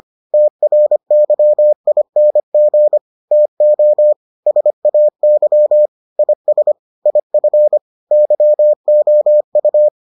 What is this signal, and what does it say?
Morse code, heard as a keyed tone, about 25 wpm, 600 Hz. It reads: TRYING TO SAY IS IF YOU